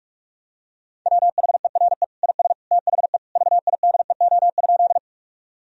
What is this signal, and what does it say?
Morse code, heard as a keyed tone, about 46 wpm, 700 Hz. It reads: WHERE IS THE VIDEO?
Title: Sentences from Top 1000 Words - Review Part 4 - 45wpm